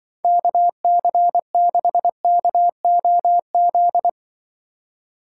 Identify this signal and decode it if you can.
Morse code, heard as a keyed tone, about 25 wpm, 700 Hz. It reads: KC6KOZ
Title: Speed Racing - USA Call Signs 20wpm